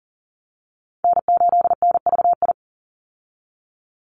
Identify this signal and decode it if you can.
Morse code, heard as a keyed tone, about 40 wpm, 700 Hz. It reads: N8NVI
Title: USA Call Signs 40wpm